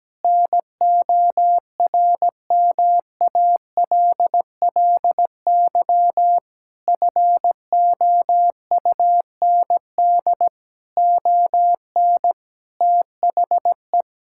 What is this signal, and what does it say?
Morse code, heard as a keyed tone, about 17 wpm, 700 Hz. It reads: NORMALLY FOUND ON THE
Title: Sentences from Top 1000 Words - Review Part 6 - 17wpm